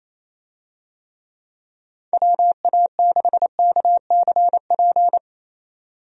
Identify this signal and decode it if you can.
Morse code, heard as a keyed tone, about 28 wpm, 700 Hz. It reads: WA6KCP
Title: USA Call Signs - 6 Characters - 28wpm